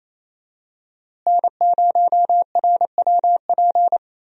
Morse code, heard as a keyed tone, about 28 wpm, 700 Hz. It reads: N0RWP
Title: USA Call Signs - 5 Characters - Repeated 2x - 28wpm